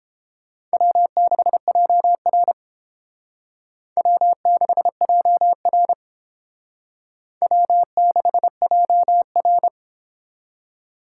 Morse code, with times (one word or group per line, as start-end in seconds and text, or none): 0.7-2.5 W6JR
4.0-5.9 W6JR
7.4-9.7 W6JR